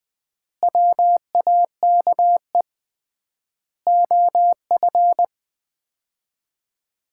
Morse code, heard as a keyed tone, about 20 wpm, 700 Hz. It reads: WAKE OF
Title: 3x Word-Spacing - Sets of 4 Words - 20wpm